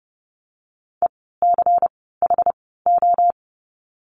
Morse code, heard as a keyed tone, about 30 wpm, 700 Hz. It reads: E C H O
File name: Lesson 16f - Letter c - Words 30wpm